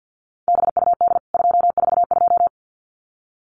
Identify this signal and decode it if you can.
Morse code, heard as a keyed tone, about 50 wpm, 700 Hz. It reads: BUD 242
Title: CWT Exchange 50wpm